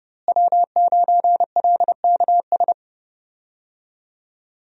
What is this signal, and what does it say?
Morse code, heard as a keyed tone, about 30 wpm, 700 Hz. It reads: W9LKS
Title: USA Call Signs - 5 Characters - Repeated 2x - 30wpm